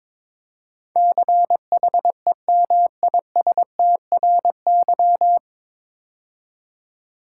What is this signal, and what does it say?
Morse code, heard as a keyed tone, about 22 wpm, 700 Hz. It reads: CHEMISTRY
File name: 9 Letter Words - First 3 Letters Given - 22wpm